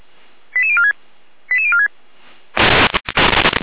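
Five-tone selective call (ZVEI), heard as quick runs of five tones, tones of about 70 ms. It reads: 88046 88046
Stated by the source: ZVEI five-tone selective call received on an SDR by Feuerwehr Orlamünde.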